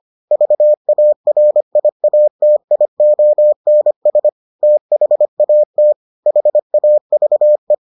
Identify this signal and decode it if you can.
Morse code, heard as a keyed tone, about 25 wpm, 600 Hz. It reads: VARIATIONS THAT HAVE